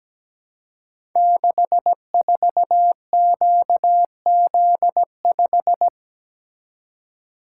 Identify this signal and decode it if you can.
Morse code, heard as a keyed tone, about 17 wpm, 700 Hz. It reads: 64QZ5
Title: Code Groups - 5 Letter:Numbers - 17wpm